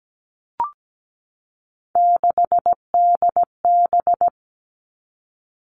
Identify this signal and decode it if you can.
Morse code, heard as a keyed tone, about 17 wpm, 700 Hz. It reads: 6DB
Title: Three Challenging Letter-Number Combinations 17wpm